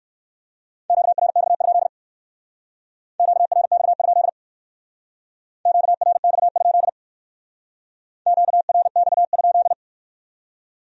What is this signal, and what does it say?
Morse code, heard as a keyed tone, about 56 wpm, 700 Hz. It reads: QRX? QRX? QRX? QRX?
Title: Speed-Racing Q-Codes - 45wpm